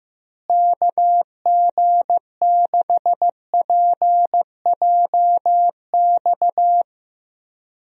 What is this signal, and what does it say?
Morse code, heard as a keyed tone, about 15 wpm, 700 Hz. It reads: KG6PJX
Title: USA Call Signs - 6 Characters - Repeated 2x - 15wpm